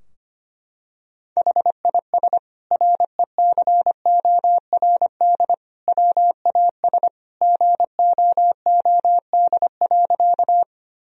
Morse code, heard as a keyed tone, about 25 wpm, 700 Hz. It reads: HIS RECORD WAS GOOD.
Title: Sentences from Top 1000 Words - Review Part 3 - 25wpm